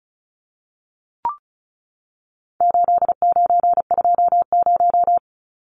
Morse code, heard as a keyed tone, about 34 wpm, 700 Hz. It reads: 8920
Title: Numbers - 4-Digit 35wpm